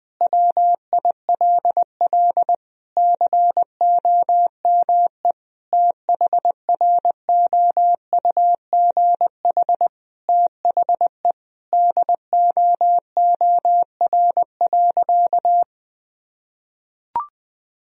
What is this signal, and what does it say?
Morse code, heard as a keyed tone, about 20 wpm, 700 Hz. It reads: WILL COME THROUGH THE DOOR.
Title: Sentences from Top 1000 Words - Review Part 1 - 20wpm